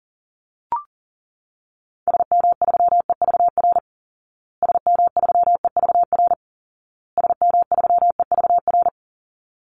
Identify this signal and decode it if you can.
Morse code, heard as a keyed tone, about 40 wpm, 700 Hz. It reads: SM3EVR SM3EVR SM3EVR
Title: Call Signs Repeated 2x - 40wpm